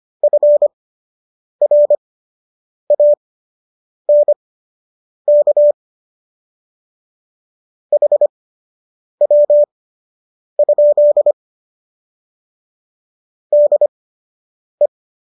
Morse code, heard as a keyed tone, about 25 wpm, spaced as for 8 wpm, 590 Hz. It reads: FRANK HW? DE